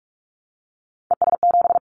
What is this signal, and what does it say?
Morse code, heard as a keyed tone, about 44 wpm, 700 Hz. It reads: ES7
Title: Callsign Prefixes for Top 100 DXCC - 45wpm